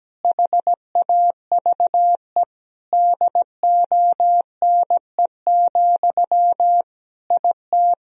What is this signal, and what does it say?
Morse code, heard as a keyed tone, about 17 wpm, 700 Hz. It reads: HAVE DONE, IT
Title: Sentences from Top 1000 Words - Review Part 2 - 17wpm